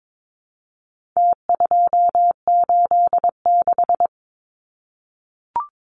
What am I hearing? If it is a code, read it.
Morse code, heard as a keyed tone, about 22 wpm, 700 Hz. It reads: T286
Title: Numbers - Contest Style - 4-Digit 22wpm